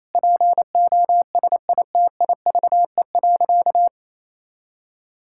Morse code, heard as a keyed tone, about 28 wpm, 700 Hz. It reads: POSITIVE.
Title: Sentences from Top 1000 Words - Review Part 5 - 28wpm